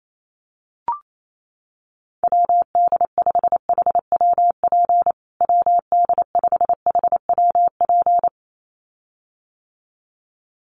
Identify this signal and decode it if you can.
Morse code, heard as a keyed tone, about 28 wpm, 700 Hz. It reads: WD5HWP WD5HWP
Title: USA Call Signs Repeated 2x 28wpm